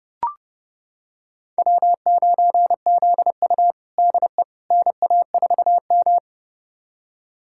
Morse code, heard as a keyed tone, about 30 wpm, 700 Hz. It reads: W9ZU DE NA4M
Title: USA Callsign DE Callsign - 4 Characters - 30wpm